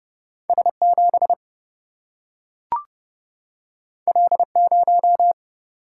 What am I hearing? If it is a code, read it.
Morse code, heard as a keyed tone, about 30 wpm, 700 Hz. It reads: S 7 L 0
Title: Single-Letter Single-Number 30wpm